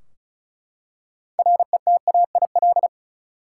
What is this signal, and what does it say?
Morse code, heard as a keyed tone, about 36 wpm, 700 Hz. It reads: RETAIL
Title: Practice QSO - Exchange 3 - More Info - 35wpm